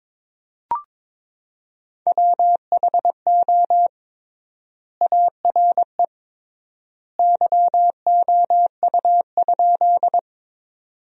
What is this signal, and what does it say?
Morse code, heard as a keyed tone, about 22 wpm, 700 Hz. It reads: WHO ARE YOU?
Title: 3x Word-Spacing - Sentences from Top 100 Words - 22wpm